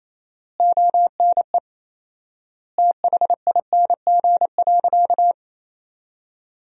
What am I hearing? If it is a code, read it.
Morse code, heard as a keyed tone, about 28 wpm, 700 Hz. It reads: ONE THING.
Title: 4x Word-Spacing - Sentences from Top 300 Words - 28wpm